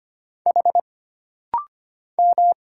Morse code, heard as a keyed tone, about 25 wpm, 700 Hz. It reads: H M
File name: Single Character (Letter:Number:Punctuation:Prosign) - Rapid-Fire - 25wpm